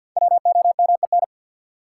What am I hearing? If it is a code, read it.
Morse code, heard as a keyed tone, about 53 wpm, 700 Hz. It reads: WOMEN
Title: Sentences from Top 1300 Words - New Word Drill - 50wpm